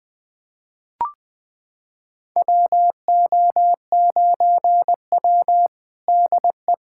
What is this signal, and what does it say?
Morse code, heard as a keyed tone, about 20 wpm, 700 Hz. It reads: WO9W DE
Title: USA Callsign DE Callsign - 4 Characters - 20wpm